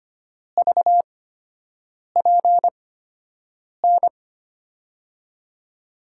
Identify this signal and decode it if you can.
Morse code, heard as a keyed tone, about 25 wpm, spaced as for 7 wpm, 700 Hz. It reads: VPN
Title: USA Call Signs - 5 Characters - 8x Character Spacing - 25wpm